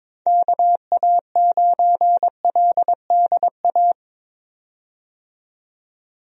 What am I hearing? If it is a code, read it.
Morse code, heard as a keyed tone, about 22 wpm, 700 Hz. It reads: KA9LDA
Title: USA Call Signs - 6 Characters - 22wpm